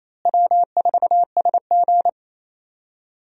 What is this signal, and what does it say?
Morse code, heard as a keyed tone, about 28 wpm, 700 Hz. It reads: W4SG
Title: USA Call Signs - 4 Characters - Repeated 2x - 28wpm